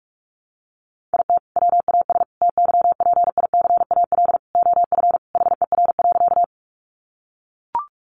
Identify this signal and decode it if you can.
Morse code, heard as a keyed tone, about 44 wpm, 700 Hz. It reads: IT WAS TYPICAL OF HER.
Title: Sentences from Top 1100 Words 45wpm